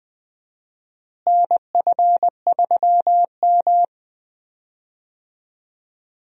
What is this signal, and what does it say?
Morse code, heard as a keyed tone, about 20 wpm, 700 Hz. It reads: NF3M
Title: USA Call Signs - 4 Characters - Repeated 2x - 20wpm